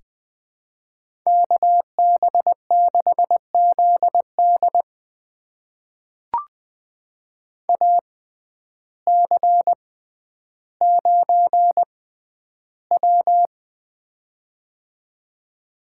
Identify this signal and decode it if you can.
Morse code, heard as a keyed tone, about 20 wpm, spaced as for 11 wpm, 700 Hz. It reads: KB6ZD AC9W
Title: USA Call Signs - 6x Character Spacing - 20wpm